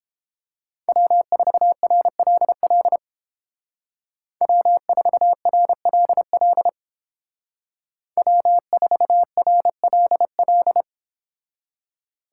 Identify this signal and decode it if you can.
Morse code, heard as a keyed tone, about 29 wpm, 700 Hz. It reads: W4RLL W4RLL W4RLL